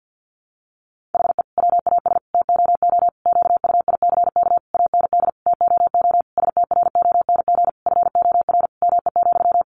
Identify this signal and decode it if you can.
Morse code, heard as a keyed tone, about 50 wpm, 700 Hz. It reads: HE WAS TOO QUICK AND TOO STRONG FOR ME,
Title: Sentences from Top 600 Words 50wpm